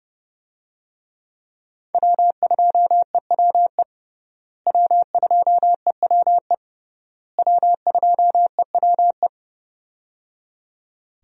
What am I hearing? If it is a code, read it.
Morse code, heard as a keyed tone, about 30 wpm, 700 Hz. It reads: W2EWE W2EWE W2EWE
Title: USA Call Signs - 5 Characters - Repeated 3x - 30wpm